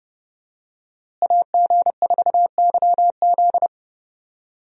Morse code, heard as a keyed tone, about 30 wpm, 700 Hz. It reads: AG4YZ
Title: USA Call Signs - 5 Characters - Repeated 2x - 30wpm